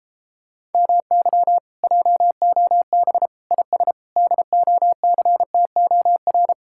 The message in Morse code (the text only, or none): MY JOB IS DOCTOR